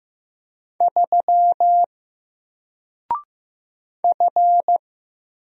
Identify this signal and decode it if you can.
Morse code, heard as a keyed tone, about 15 wpm, 700 Hz. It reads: 3 F
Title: Single Letter-Number 15wpm